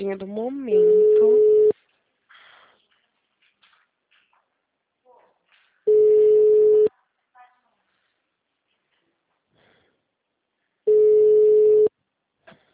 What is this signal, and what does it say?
Ringback tone (Europe, 425 Hz), with speech.